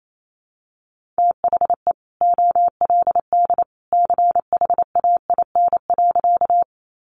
Morse code, heard as a keyed tone, about 28 wpm, 700 Hz. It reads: THE OLD CHAIN.
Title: Sentences from Top 1500 Words 28wpm